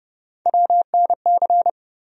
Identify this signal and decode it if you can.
Morse code, heard as a keyed tone, about 30 wpm, 700 Hz. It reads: WNC
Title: USA Call Sign Postfixes - 3 Letters 30wpm